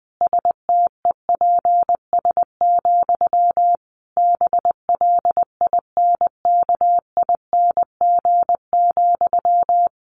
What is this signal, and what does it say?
Morse code, heard as a keyed tone, about 20 wpm, 700 Hz. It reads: STEPS, BLINKING,
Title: A Sound Of Thunder 20wpm